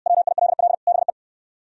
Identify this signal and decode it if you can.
Morse code, heard as a keyed tone, about 67 wpm, 700 Hz. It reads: WILL BE